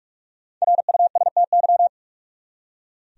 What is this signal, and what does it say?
Morse code, heard as a keyed tone, about 46 wpm, 700 Hz. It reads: RUSTY
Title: USA Common Names - Max 5 Letters - 45wpm